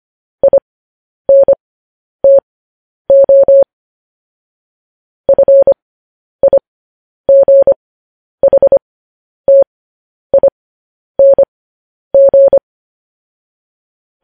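Morse code, heard as a keyed tone, about 25 wpm, spaced as for 10 wpm, 550 Hz. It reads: INTO FIGHTING
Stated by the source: Morse code news headlines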